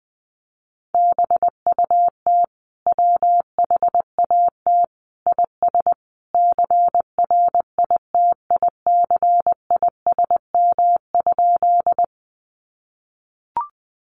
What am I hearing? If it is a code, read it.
Morse code, heard as a keyed tone, about 20 wpm, 700 Hz. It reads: BUT WHAT IS CRITICISM?
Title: Sentences from Top 1300 Words 20wpm